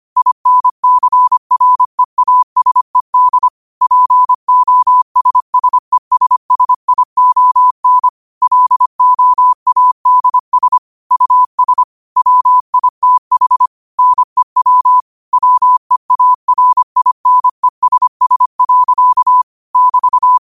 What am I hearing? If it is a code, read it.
Morse code, heard as a keyed tone, about 25 wpm, 1000 Hz. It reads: INCREASED POSSESSION LOADS US WITH NEW WEARINESS. <BT>
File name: qotdfiles / 25wpm